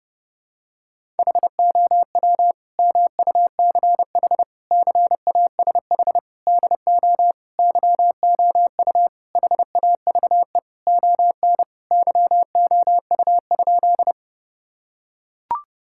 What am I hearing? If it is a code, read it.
Morse code, heard as a keyed tone, about 32 wpm, 700 Hz. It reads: HOW MUCH CASH DO YOU HAVE ON YOU?
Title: Sentences from Top 1000 Words - Review Part 5 - 30wpm